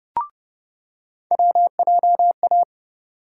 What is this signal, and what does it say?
Morse code, heard as a keyed tone, about 30 wpm, 700 Hz. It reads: WJA
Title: USA Call Sign Postfixes - 3 Letters 30wpm